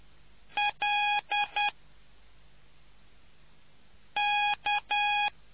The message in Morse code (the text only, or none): LK